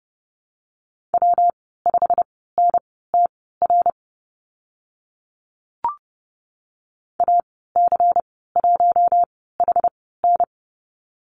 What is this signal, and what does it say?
Morse code, heard as a keyed tone, about 30 wpm, spaced as for 17 wpm, 700 Hz. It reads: W5NTR AC1HN
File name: Lesson 19g - Letter w - Call Signs 30wpm